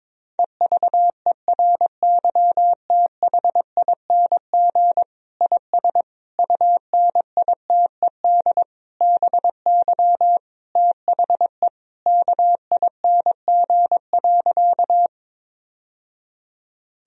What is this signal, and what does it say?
Morse code, heard as a keyed tone, about 22 wpm, 700 Hz. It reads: EVERYTHING IS UNITED BY THE KING.
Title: Sentences from Top 1000 Words - Review Part 4 - 22wpm